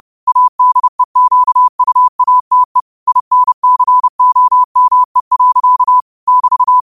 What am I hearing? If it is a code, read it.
Morse code, heard as a keyed tone, about 30 wpm, 1000 Hz. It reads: ADEQUATE INCOME. <BT>